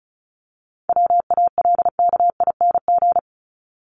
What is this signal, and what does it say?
Morse code, heard as a keyed tone, about 34 wpm, 700 Hz. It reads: WALKING
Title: ING Suffix - Top 100 Words with Base Words 35wpm